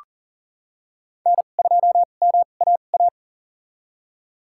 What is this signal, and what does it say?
Morse code, heard as a keyed tone, about 40 wpm, spaced as for 29 wpm, 700 Hz. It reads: N2MAA
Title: USA Call Signs - 5 Characters - 2x Character Spacing - 40wpm